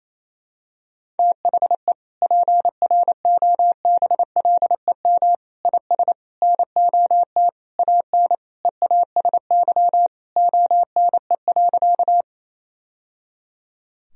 Morse code, heard as a keyed tone, about 28 wpm, 700 Hz. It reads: THE PROBLEM IS NOT AN EASY ONE.
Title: Sentences from Top 1000 Words - Review Part 2 - 28wpm